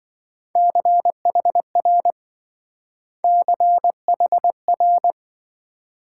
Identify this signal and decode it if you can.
Morse code, heard as a keyed tone, about 26 wpm, 700 Hz. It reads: CHR CHR